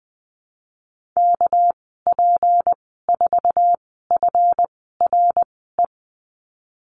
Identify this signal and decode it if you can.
Morse code, heard as a keyed tone, about 20 wpm, spaced as for 15 wpm, 700 Hz. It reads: KP4FRE
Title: 200 Lesson 32g - Letter k - Call Signs 20wpm